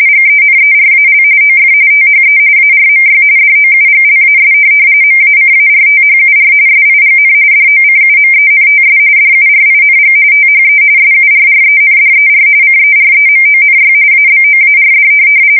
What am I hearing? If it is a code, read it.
RTTY, about 45.45 baud. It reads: GENERAL OF THE ARMY DOUGLAS MACARTHUR SIGN DOCUMENTS DURING THE SURRENDER CEREMONY ABOARD
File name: fsk170 / USS-Missouri